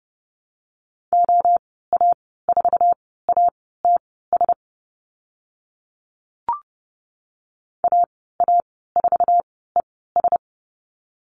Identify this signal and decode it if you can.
Morse code, heard as a keyed tone, about 30 wpm, spaced as for 17 wpm, 700 Hz. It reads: OA4ATS AA4ES
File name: Lesson 14g - Number 2 - Call Signs 30wpm